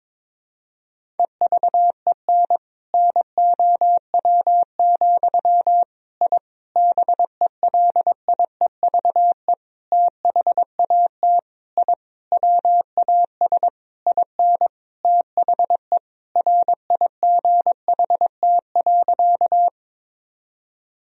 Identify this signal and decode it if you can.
Morse code, heard as a keyed tone, about 22 wpm, 700 Hz. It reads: EVEN NOW, I BELIEVE THAT I WAS IN THE RIGHT.
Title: Sentences from Top 1000 Words - Review Part 1 - 22wpm